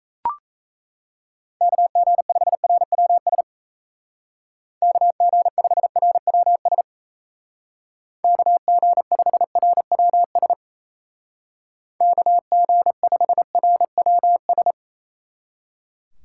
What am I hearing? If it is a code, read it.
Morse code, heard as a keyed tone, about 35 wpm, 700 Hz. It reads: KG5RWS KG5RWS KG5RWS KG5RWS